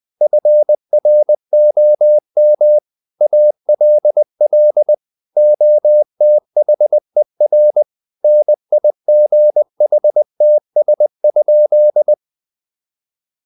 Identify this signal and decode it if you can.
Morse code, heard as a keyed tone, about 20 wpm, 600 Hz. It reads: FROM ALL OTHER NIGHTS?